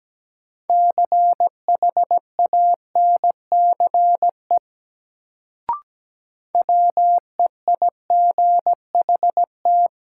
Morse code, heard as a keyed tone, about 17 wpm, 700 Hz. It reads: CHANCE WEIGHT
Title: Sentences from Top 800 Words - New Word Drill - 17wpm